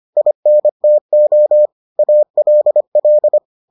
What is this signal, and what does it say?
Morse code, heard as a keyed tone, about 25 wpm, 600 Hz. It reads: INTO ALL